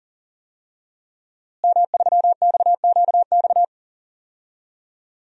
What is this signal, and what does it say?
Morse code, heard as a keyed tone, about 37 wpm, 700 Hz. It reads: M3XQX
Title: DXCC-100 Call Signs - 5 Characters - Repeated 3x - 40wpm